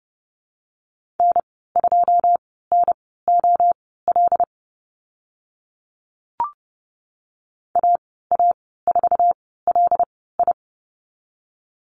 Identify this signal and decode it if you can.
Morse code, heard as a keyed tone, about 30 wpm, spaced as for 17 wpm, 700 Hz. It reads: N2NOL AA4LI
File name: Lesson 14g - Number 2 - Call Signs 30wpm